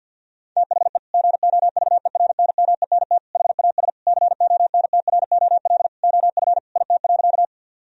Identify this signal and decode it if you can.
Morse code, heard as a keyed tone, about 50 wpm, 700 Hz. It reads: THE GOVERNMENT HAS CONTROL OF IT.
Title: Sentences from Top 1000 Words - Review Part 3 - 50wpm